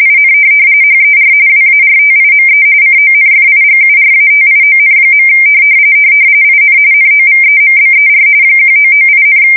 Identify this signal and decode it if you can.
RTTY, about 45.45 baud. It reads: IN HER WARTIME CONFIGURATION. IN NOVEMBER 1998,